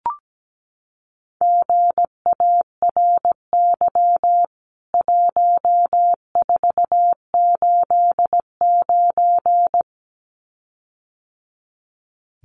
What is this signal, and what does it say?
Morse code, heard as a keyed tone, about 17 wpm, 700 Hz. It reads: GARY 1489
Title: CWT Exchange 17wpm